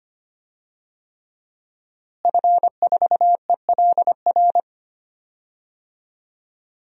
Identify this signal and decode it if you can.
Morse code, heard as a keyed tone, about 25 wpm, 700 Hz. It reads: F4ELR
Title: DXCC-100 Call Signs - 5 Characters - Repeated 3x - 25wpm